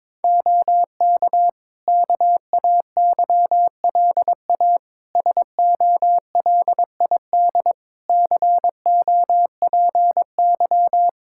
Morse code, heard as a keyed tone, about 22 wpm, 700 Hz. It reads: OK KAYLA SOLID COPY
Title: K1USN SST QSO Practice 22wpm